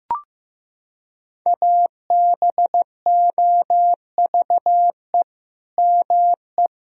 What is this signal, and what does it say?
Morse code, heard as a keyed tone, about 15 wpm, 700 Hz. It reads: ABOVE ME
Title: Sentences from Top 1000 Words - Review Part 3 - 15wpm